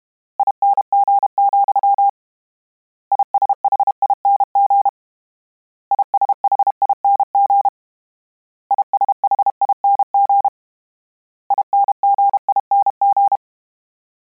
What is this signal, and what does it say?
Morse code, heard as a keyed tone, about 32 wpm, 800 Hz. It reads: ING, ISHING ISHING ISHING INGING
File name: Combinations / Combinations-mp3s / Combinations@30